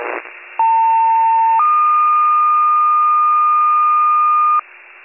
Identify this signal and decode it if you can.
Two-tone page, heard as a one-second tone, then a three-second tone, about 900 then 1220 Hz. It is heard from 0.6 to 4.6 s.